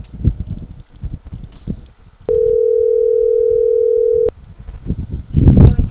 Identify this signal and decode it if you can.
Ringback tone (North America), 440 + 480 Hz.